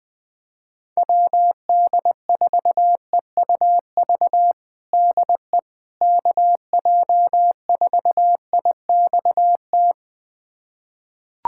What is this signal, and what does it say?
Morse code, heard as a keyed tone, about 20 wpm, 700 Hz. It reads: WD4EUV DE KJ4IXT
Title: USA Callsign DE Callsign - 6 Characters - 20wpm